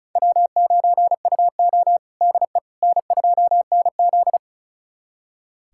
Morse code, heard as a keyed tone, about 36 wpm, 700 Hz. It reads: W9UO DE N2NZ
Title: USA Callsign DE Callsign - 4 Characters - 35wpm